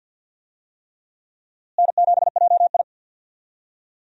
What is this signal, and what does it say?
Morse code, heard as a keyed tone, about 51 wpm, 700 Hz. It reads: N7JI
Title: USA Call Signs - 4 Characters - Repeated 3x - 50wpm